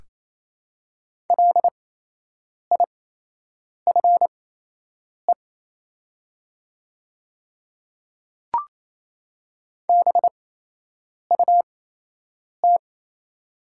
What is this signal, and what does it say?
Morse code, heard as a keyed tone, about 28 wpm, spaced as for 8 wpm, 700 Hz. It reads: LIFE BUT
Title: 8x Character Spacing - Top 300 Words - 28wpm